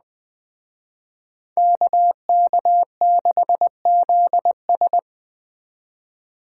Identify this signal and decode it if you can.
Morse code, heard as a keyed tone, about 20 wpm, 700 Hz. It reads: KK6ZS